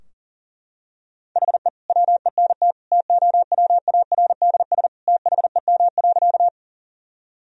Morse code, heard as a keyed tone, about 40 wpm, 700 Hz. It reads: HE WENT TOWARDS THEM.
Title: Sentences from Top 1000 Words - Review Part 2 - 40wpm